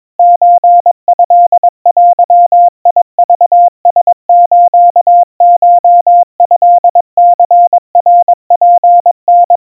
Morse code (Japanese Wabun code), heard as a keyed tone, about 22 wpm, 700 Hz. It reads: ソトデクラスコトニナツタ
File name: 80 cpm 700 Hz Crispy